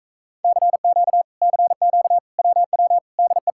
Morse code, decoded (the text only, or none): CQ CQ WW DE